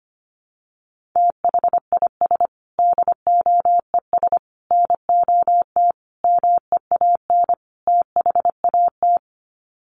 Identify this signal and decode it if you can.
Morse code, heard as a keyed tone, about 25 wpm, 700 Hz. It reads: THIS DOES NOT MEAN THAT
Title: Sets of 5 Words Encore2 25wpm